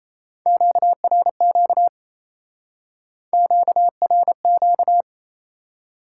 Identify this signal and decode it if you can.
Morse code, heard as a keyed tone, about 36 wpm, 700 Hz. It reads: QRQ QRQ